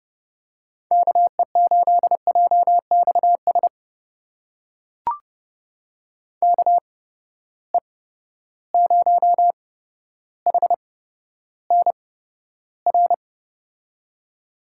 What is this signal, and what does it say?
Morse code, heard as a keyed tone, about 30 wpm, spaced as for 13 wpm, 700 Hz. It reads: KE8JXS KE0HNR
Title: USA Call Signs - 6 Characters - 8x Character Spacing - 30wpm